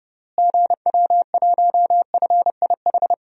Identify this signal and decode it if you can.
Morse code, heard as a keyed tone, about 30 wpm, 700 Hz. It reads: GW1FIH
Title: DXCC-100 Call Signs - 6 Characters - Repeated 3x - 30wpm